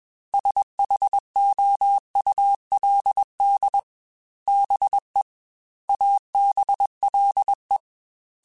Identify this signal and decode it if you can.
Morse code, heard as a keyed tone, about 21 wpm, 800 Hz. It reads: SHOULD BE ABLE
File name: Technical Subjects / Technical Subjects-mp3s / OR Gate @20